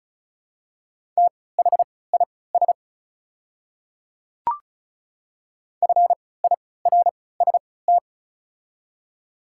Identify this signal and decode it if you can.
Morse code, heard as a keyed tone, about 36 wpm, spaced as for 20 wpm, 700 Hz. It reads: THIS FIRST
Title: 3x Character Spacing - Top 100 Words - 35wpm